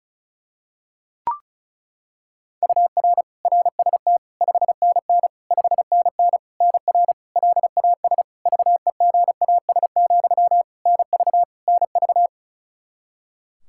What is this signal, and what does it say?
Morse code, heard as a keyed tone, about 36 wpm, 700 Hz. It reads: UR RST 5NN 5NN NR LAS VEGAS, NV NV
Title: K1USN SST QSO Practice 35wpm